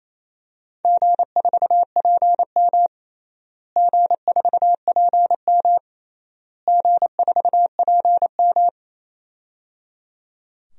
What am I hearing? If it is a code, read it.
Morse code, heard as a keyed tone, about 28 wpm, 700 Hz. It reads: G4PM G4PM G4PM
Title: DXCC-100 Call Signs - 4 Characters - Repeated 3x - 28wpm